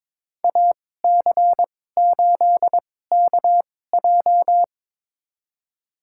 Morse code, heard as a keyed tone, about 22 wpm, spaced as for 16 wpm, 700 Hz. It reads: AC8KJ
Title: USA Call Signs - 5 Characters - 2x Character Spacing - 22wpm